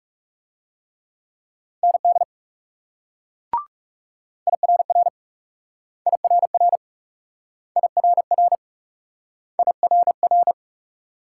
Morse code, heard as a keyed tone, about 38 wpm, 700 Hz. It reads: ND IRR IRR IRR IRR